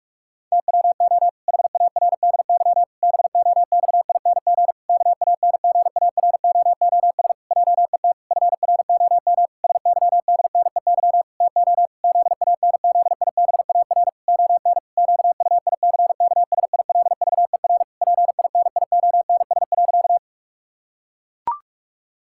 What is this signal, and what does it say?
Morse code, heard as a keyed tone, about 46 wpm, 700 Hz. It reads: TWO HARDY BOXING KANGAROOS JET FROM SYDNEY TO ZANZIBAR ON QUICKSILVER PINIONS.
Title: Pangram Challenge v2 - 45wpm